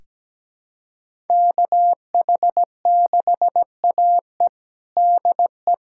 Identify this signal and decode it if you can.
Morse code, heard as a keyed tone, about 17 wpm, 700 Hz. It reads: KH6AE DE
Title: USA Callsign DE Callsign - 5 Characters - 17wpm